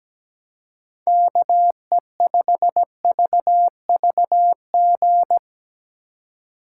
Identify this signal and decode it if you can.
Morse code, heard as a keyed tone, about 18 wpm, 700 Hz. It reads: KE5VVG
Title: Speed Racing - USA 2x3 Call Signs 17wpm